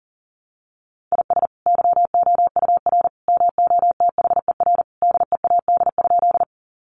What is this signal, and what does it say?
Morse code, heard as a keyed tone, about 40 wpm, 700 Hz. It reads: IS YOUR MOTHER DEAD?
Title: Sentences from Top 1500 Words 40wpm